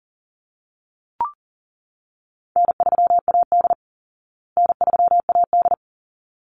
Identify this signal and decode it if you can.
Morse code, heard as a keyed tone, about 40 wpm, 700 Hz. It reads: N3AD N3AD
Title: Easy Call Signs Repeated 2x - 40wpm